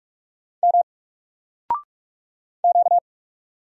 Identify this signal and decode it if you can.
Morse code, heard as a keyed tone, about 44 wpm, 700 Hz. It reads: M Q
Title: Single Letter-Number Alt 45wpm